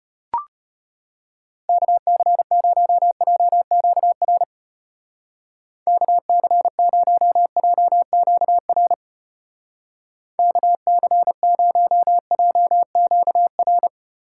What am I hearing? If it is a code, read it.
Morse code, heard as a keyed tone, about 33 wpm, 700 Hz. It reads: KC0JQR KC0JQR KC0JQR